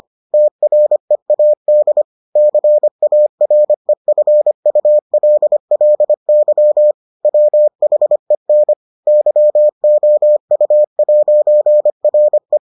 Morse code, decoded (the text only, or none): TREAD CAREFULLY WHEN YOU'RE